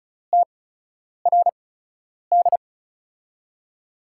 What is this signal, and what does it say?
Morse code, heard as a keyed tone, about 36 wpm, spaced as for 10 wpm, 700 Hz. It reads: TRD
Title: USA Call Signs - 8x Character Spacing - 35wpm